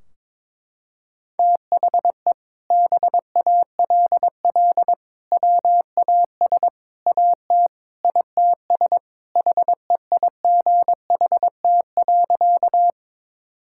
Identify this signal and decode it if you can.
Morse code, heard as a keyed tone, about 21 wpm, 700 Hz. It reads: THE BALL WAS AT ITS HEIGHT.
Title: Sentences from Top 1600 Words 22wpm